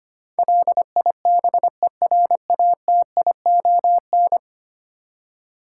Morse code, heard as a keyed tone, about 25 wpm, 700 Hz. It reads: LIBERATION
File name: Ten Letter Words - 25wpm